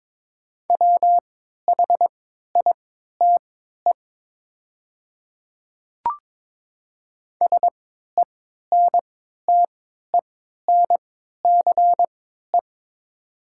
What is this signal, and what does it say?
Morse code, heard as a keyed tone, about 22 wpm, spaced as for 13 wpm, 700 Hz. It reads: WHITE SENTENCE
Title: 3x Character Spacing - Top 500 Words - 22wpm